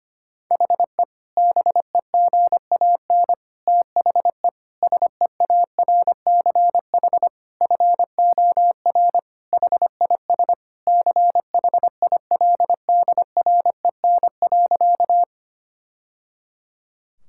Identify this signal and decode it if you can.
Morse code, heard as a keyed tone, about 25 wpm, 700 Hz. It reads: HE BEGAN THE SEARCH FOR HIS CHILDREN.
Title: Sentences from Top 1000 Words - Review Part 5 - 25wpm